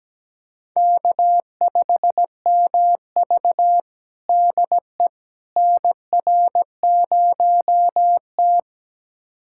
Morse code, heard as a keyed tone, about 17 wpm, 700 Hz. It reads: K5MV DE NR0T
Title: USA Callsign DE Callsign - 4 Characters - 17wpm